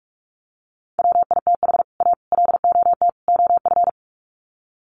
Morse code, heard as a keyed tone, about 44 wpm, 700 Hz. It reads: WITH A LOT OF
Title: Sets of 4 Words 45wpm